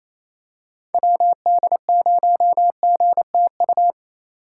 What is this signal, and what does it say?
Morse code, heard as a keyed tone, about 28 wpm, 700 Hz. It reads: WD0GTU
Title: USA Call Signs - 3x Character Spacing - 28wpm